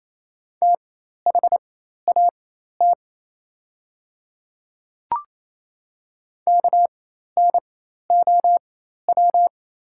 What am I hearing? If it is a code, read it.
Morse code, heard as a keyed tone, about 27 wpm, spaced as for 13 wpm, 700 Hz. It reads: THAT KNOW
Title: 4x Character Spacing - Top 100 Words - 28wpm